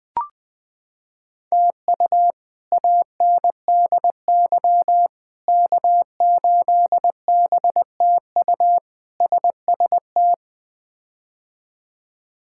Morse code, heard as a keyed tone, about 20 wpm, 700 Hz. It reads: TU ANDY K8BTU SST
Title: K1USN SST QSO Practice - 20wpm